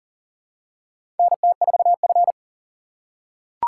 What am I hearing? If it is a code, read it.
Morse code, heard as a keyed tone, about 37 wpm, 700 Hz. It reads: NT4F
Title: USA Call Signs - 4 Characters - 2x Character Spacing - 40wpm